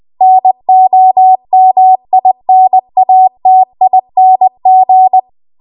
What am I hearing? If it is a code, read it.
Morse code, heard as a keyed tone, about 20 wpm, 750 Hz. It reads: NOMINATING
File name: Archive / 20 WPM / 20WPM